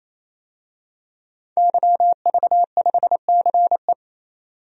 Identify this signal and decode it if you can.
Morse code, heard as a keyed tone, about 28 wpm, 700 Hz. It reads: YV5CE
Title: DXCC-100 Call Signs - 5 Characters - 28wpm